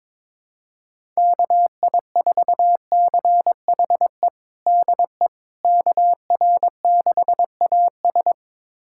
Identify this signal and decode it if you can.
Morse code, heard as a keyed tone, about 22 wpm, 700 Hz. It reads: KI4CHE DE KR6AS
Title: USA Callsign DE Callsign 22wpm